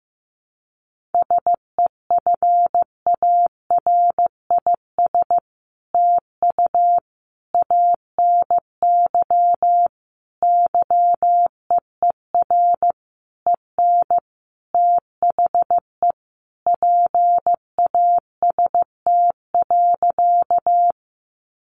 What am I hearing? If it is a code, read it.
Morse code, heard as a keyed tone, about 15 wpm, 700 Hz. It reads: SEFARIS TU ANY YEER EN THE PAST.
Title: A Sound Of Thunder 15wpm